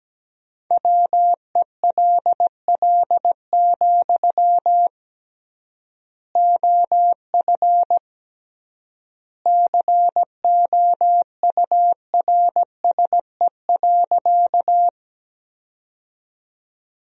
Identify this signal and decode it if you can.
Morse code, heard as a keyed tone, about 17 wpm, 700 Hz. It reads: WELL, OF COURSE.
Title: 3x Word-Spacing - Sentences from Top 500 Words - 17wpm